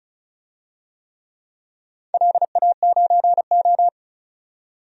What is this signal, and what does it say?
Morse code, heard as a keyed tone, about 34 wpm, 700 Hz. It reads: LA9O
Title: DXCC-100 Call Signs - 4 Characters - 35wpm